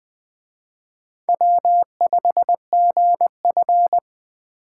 Morse code, heard as a keyed tone, about 21 wpm, 700 Hz. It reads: W5GF